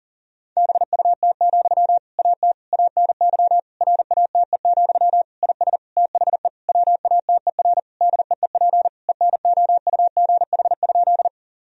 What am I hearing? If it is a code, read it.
Morse code, heard as a keyed tone, about 40 wpm, 700 Hz. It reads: BUT, AT ANY RATE, IS THE WATER DEEP ENOUGH?
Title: Sentences from Top 1000 Words - Review Part 4 - 40wpm